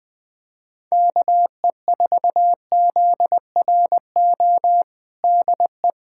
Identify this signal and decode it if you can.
Morse code, heard as a keyed tone, about 21 wpm, 700 Hz. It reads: KE4ZRO DE
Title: USA Callsign DE Callsign 20wpm